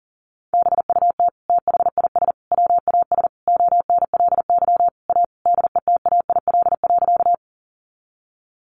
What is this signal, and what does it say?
Morse code, heard as a keyed tone, about 40 wpm, 700 Hz. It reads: BUT THIS WAS ONLY A DETAIL.
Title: Sentences from Top 1500 Words 40wpm